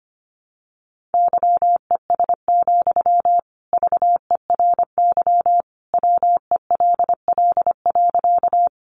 Morse code, heard as a keyed tone, about 25 wpm, 700 Hz. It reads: YES, VERY WELL.
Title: Sentences from Top 500 Words 25wpm